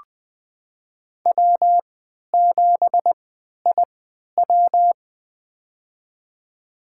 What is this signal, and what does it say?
Morse code, heard as a keyed tone, about 21 wpm, spaced as for 11 wpm, 700 Hz. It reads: W7IW
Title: USA Call Signs - 3x Character Spacing - 20wpm